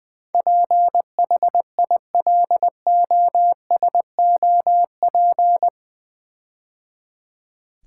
Morse code, heard as a keyed tone, about 20 wpm, 700 Hz. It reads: PHILOSOP
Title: Build A Word - 4 Letters At A Time - 11-Letter Words - 20wpm